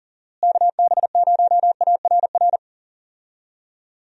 Morse code, heard as a keyed tone, about 38 wpm, 700 Hz. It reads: KB0ARR
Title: USA Call Signs - 6 Characters - Repeated 3x - 40wpm